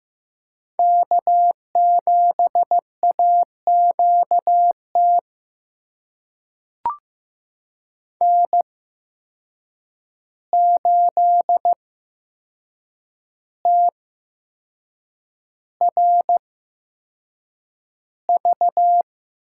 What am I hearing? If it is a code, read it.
Morse code, heard as a keyed tone, about 15 wpm, spaced as for 6 wpm, 700 Hz. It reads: K7AQT N8TRV
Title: USA Call Signs - 5 Characters - 8x Character Spacing - 15wpm